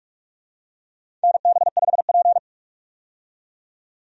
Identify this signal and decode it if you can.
Morse code, heard as a keyed tone, about 43 wpm, 700 Hz. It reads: NB5P
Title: USA Call Signs - 4 Characters - 45wpm